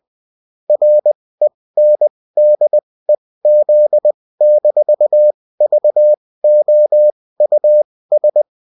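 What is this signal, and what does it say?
Morse code, heard as a keyed tone, about 20 wpm, spaced as for 16 wpm, 600 Hz. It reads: RENDEZ-VOUS
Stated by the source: Morse code bulletin of a French amateur CW union, sent as text and in CW.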